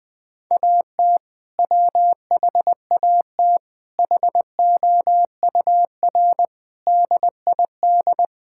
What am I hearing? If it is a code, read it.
Morse code, heard as a keyed tone, about 21 wpm, 700 Hz. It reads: AT WHAT HOUR DID
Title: Sentences from Top 1600 Words 20wpm